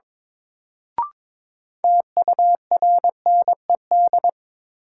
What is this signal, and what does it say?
Morse code, heard as a keyed tone, about 22 wpm, 700 Hz. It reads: TURNED
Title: Sets of 4 Words - New Word Drill - 22wpm